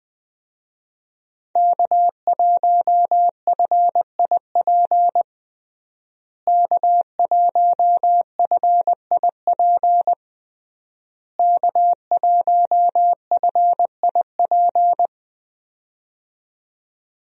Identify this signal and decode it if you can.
Morse code, heard as a keyed tone, about 21 wpm, 700 Hz. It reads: K1FIP K1FIP K1FIP
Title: USA Call Signs - 5 Characters - Repeated 3x - 20wpm